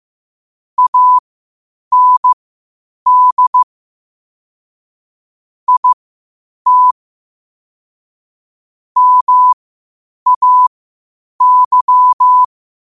Morse code, heard as a keyed tone, about 15 wpm, spaced as for 8 wpm, 1000 Hz. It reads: AND IT MAY
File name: qotdfiles / 8wpm